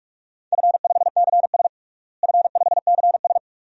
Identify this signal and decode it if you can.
Morse code, heard as a keyed tone, about 46 wpm, 700 Hz. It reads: F5CS F5CS
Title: DXCC-100 Call Signs - 4 Characters - Repeated 3x - 45wpm